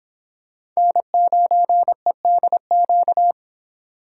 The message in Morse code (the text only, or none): N9EDQ